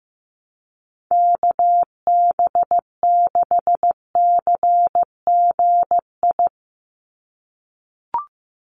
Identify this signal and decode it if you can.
Morse code, heard as a keyed tone, about 15 wpm, 700 Hz. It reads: KB6CGI
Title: USA Call Signs 15wpm